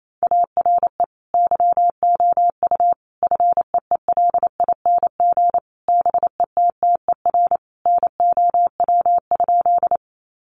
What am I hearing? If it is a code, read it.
Morse code, heard as a keyed tone, about 28 wpm, 700 Hz. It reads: ARE YOU FEELING BETTER NOW?
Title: Sentences from Top 1200 Words 28wpm